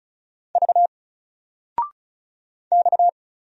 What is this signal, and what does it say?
Morse code, heard as a keyed tone, about 36 wpm, 700 Hz. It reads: V X
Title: Single Letter-Number Alt 35wpm